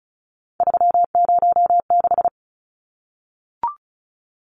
Morse code, heard as a keyed tone, about 34 wpm, 700 Hz. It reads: 306
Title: Numbers - 3-Digit 35wpm